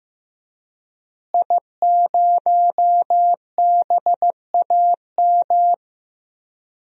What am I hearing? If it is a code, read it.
Morse code, heard as a keyed tone, about 15 wpm, 700 Hz. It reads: I0BAM
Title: DXCC-100 Call Signs - 5 Characters - Repeated 3x - 15wpm